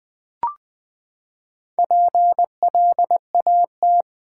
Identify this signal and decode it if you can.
Morse code, heard as a keyed tone, about 20 wpm, 700 Hz. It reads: PLAT
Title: Build A Word - 4 Letters At A Time - 8-Letter Words - 20wpm